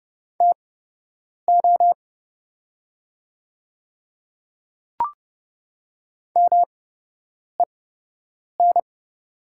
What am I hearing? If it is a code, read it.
Morse code, heard as a keyed tone, about 30 wpm, spaced as for 8 wpm, 700 Hz. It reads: TO MEN
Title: 8x Character Spacing - Top 200 Words - 30wpm